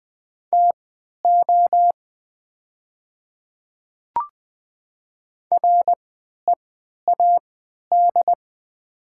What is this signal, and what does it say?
Morse code, heard as a keyed tone, about 20 wpm, spaced as for 11 wpm, 700 Hz. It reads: TO READ
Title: 3x Character Spacing - Top 300 Words - 20wpm